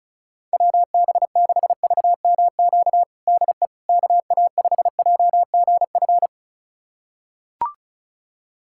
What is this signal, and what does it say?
Morse code, heard as a keyed tone, about 36 wpm, 700 Hz. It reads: WB6VMQ DE KA5JGF
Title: USA Callsign DE Callsign - 6 Characters - 35wpm